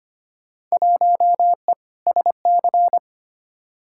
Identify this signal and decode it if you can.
Morse code, heard as a keyed tone, about 25 wpm, 700 Hz. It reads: 1E SC
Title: Field Day Exchanges - 25wpm